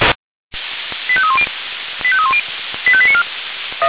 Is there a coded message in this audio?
Yes: five-tone selective call (ZVEI) from 1.1 to 3.2 s, heard as quick runs of five tones, tones of about 70 ms. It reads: 85310 85310 85894